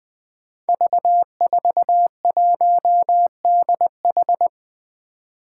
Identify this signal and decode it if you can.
Morse code, heard as a keyed tone, about 21 wpm, 700 Hz. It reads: V41DH
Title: Five Challenging Letter-Number Combinations 20wpm